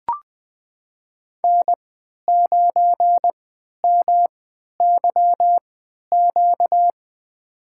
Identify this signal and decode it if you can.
Morse code, heard as a keyed tone, about 20 wpm, spaced as for 11 wpm, 700 Hz. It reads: N9MYQ
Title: USA Call Signs - 5 Characters - 3x Character Spacing - 20wpm